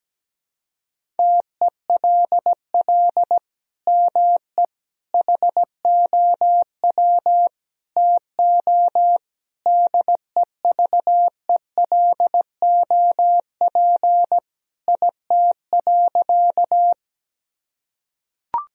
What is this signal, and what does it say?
Morse code, heard as a keyed tone, about 17 wpm, 700 Hz. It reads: TELL ME HOW TO DEVELOP IT.
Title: Sentences from Top 1000 Words - Review Part 2 - 17wpm